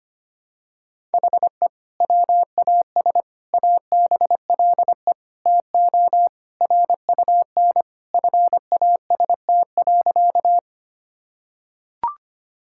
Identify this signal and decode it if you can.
Morse code, heard as a keyed tone, about 25 wpm, 700 Hz. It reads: HE WAS ABLE TO RUN FAST.
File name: practice / 25wpm / Sentences with 2-Word Phrases - 25wpm